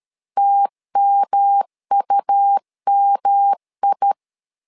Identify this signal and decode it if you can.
Morse code, heard as a keyed tone, about 13 wpm, 800 Hz. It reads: TMUMI
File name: LEZIONI / 3 MWRO / 3 M 12wpm 1541